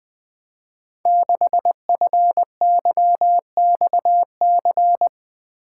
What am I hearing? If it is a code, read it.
Morse code, heard as a keyed tone, about 20 wpm, 700 Hz. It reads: 6FYXC
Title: Code Groups - 5 Letter:Numbers - 20wpm